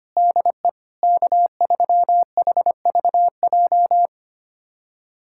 Morse code, heard as a keyed tone, about 25 wpm, 700 Hz. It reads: DE K3HVJ
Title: USA Callsign DE Callsign - 5 Characters - 25wpm